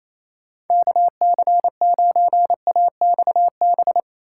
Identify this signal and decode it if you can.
Morse code, heard as a keyed tone, about 38 wpm, spaced as for 34 wpm, 700 Hz. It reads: KC9AXB
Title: Speed-Racing Common QSO Phrases - 28wpm